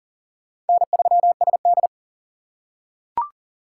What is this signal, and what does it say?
Morse code, heard as a keyed tone, about 40 wpm, 700 Hz. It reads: N3SD
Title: USA Call Signs - 4 Characters - 4x Character Spacing - 40wpm